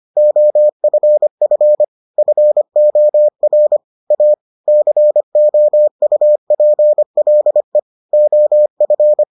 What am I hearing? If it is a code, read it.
Morse code, heard as a keyed tone, about 25 wpm, 600 Hz. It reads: OFF FOR A COUPLE OF